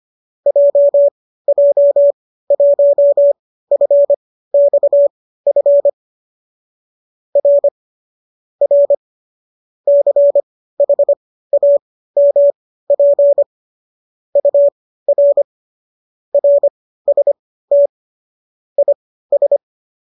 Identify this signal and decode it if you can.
Morse code, heard as a keyed tone, about 25 wpm, spaced as for 15 wpm, 570 Hz. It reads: JJ1FXF R R CHAMP UR RST IS
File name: qso208 15